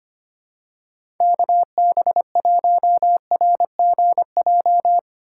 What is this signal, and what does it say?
Morse code, heard as a keyed tone, about 25 wpm, 700 Hz. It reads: KB1RGJ
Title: USA Call Signs - 6 Characters - 4x Character Spacing - 25wpm